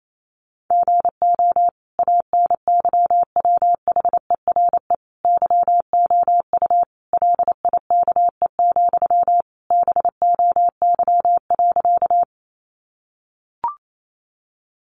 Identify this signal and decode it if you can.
Morse code, heard as a keyed tone, about 28 wpm, 700 Hz. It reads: GO ANYWHERE YOU LIKE, BOY.
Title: Sentences from Top 1400 Words 28wpm